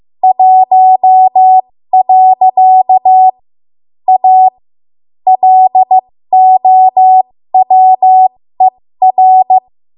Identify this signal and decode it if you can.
Morse code, heard as a keyed tone, about 15 wpm, spaced as for 13 wpm, 750 Hz. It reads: 1. A LOWER
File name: Morse / 13WPM